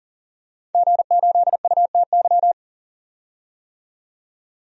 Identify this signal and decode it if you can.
Morse code, heard as a keyed tone, about 40 wpm, 700 Hz. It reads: G8UTY
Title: DXCC-100 Call Signs - 5 Characters - 40wpm